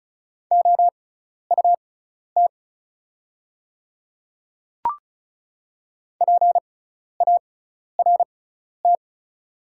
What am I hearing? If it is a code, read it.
Morse code, heard as a keyed tone, about 36 wpm, spaced as for 12 wpm, 700 Hz. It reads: OUT PART
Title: 6x Character Spacing - Top 200 Words - 35wpm